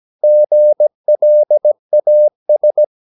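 Morse code, heard as a keyed tone, about 17 wpm, 600 Hz. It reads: GLAS
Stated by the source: Slow (QRS) Morse practice at 17 wpm, from CW med Gnister.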